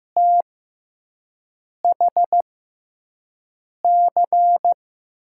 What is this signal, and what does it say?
Morse code, heard as a keyed tone, about 15 wpm, spaced as for 5 wpm, 700 Hz. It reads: THC
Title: USA Call Signs - 5 Characters - 6x Character Spacing - 15wpm